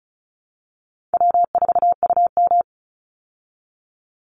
Morse code, heard as a keyed tone, about 34 wpm, 700 Hz. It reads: W4UM
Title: Easy Call Signs Repeated 3x - 35wpm